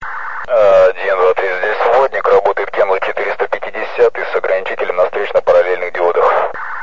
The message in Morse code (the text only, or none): none